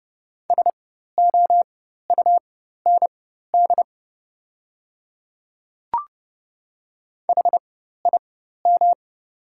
Morse code, heard as a keyed tone, about 30 wpm, spaced as for 14 wpm, 700 Hz. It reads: SOUND HIM
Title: 4x Character Spacing - Top 100 Words - 30wpm